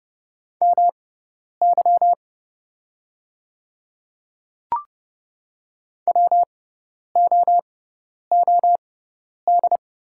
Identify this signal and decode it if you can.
Morse code, heard as a keyed tone, about 30 wpm, spaced as for 10 wpm, 700 Hz. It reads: MY WOOD
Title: 6x Character Spacing - Top 300 Words - 30wpm